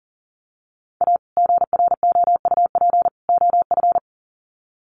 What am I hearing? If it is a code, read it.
Morse code, heard as a keyed tone, about 40 wpm, 700 Hz. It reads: A GROUP OF
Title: Sets of 3 Words 40wpm